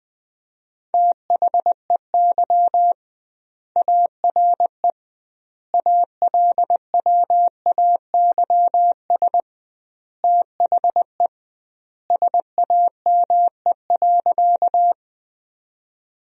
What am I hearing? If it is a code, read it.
Morse code, heard as a keyed tone, about 20 wpm, 700 Hz. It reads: THEY ARE ALWAYS THE SAME.
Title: 2x Word-Spacing - Sentences from Top 300 Words - 20wpm